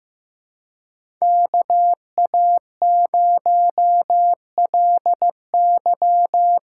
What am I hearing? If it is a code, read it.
Morse code, heard as a keyed tone, about 15 wpm, 700 Hz. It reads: KA0LY
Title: USA Call Signs - 6x Character Spacing - 15wpm